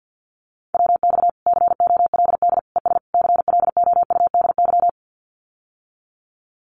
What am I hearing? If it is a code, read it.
Morse code, heard as a keyed tone, about 50 wpm, 700 Hz. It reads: WX COLD ES CLOUDY
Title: Common QSO Phrases Encore 50wpm